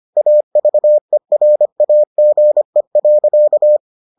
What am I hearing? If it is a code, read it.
Morse code, heard as a keyed tone, about 25 wpm, 600 Hz. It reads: AVERAGE.